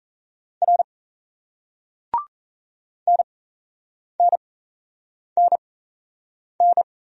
Morse code, heard as a keyed tone, about 35 wpm, spaced as for 8 wpm, 700 Hz. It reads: R NNNN
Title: Speed-Racing Single Letters 28wpm